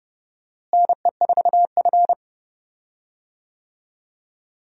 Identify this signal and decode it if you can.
Morse code, heard as a keyed tone, about 30 wpm, 700 Hz. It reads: NE4F
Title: USA Call Signs - 4 Characters - Repeated 2x - 30wpm